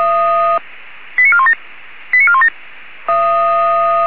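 Five-tone selective call (ZVEI), heard as quick runs of five tones, tones of about 70 ms. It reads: 89417 89417
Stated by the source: ZVEI five-tone selective call received on SDR.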